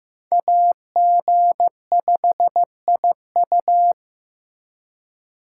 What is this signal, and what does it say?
Morse code, heard as a keyed tone, about 15 wpm, 700 Hz. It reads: AG5IU
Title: USA Call Signs - 5 Characters - 3x Character Spacing - 15wpm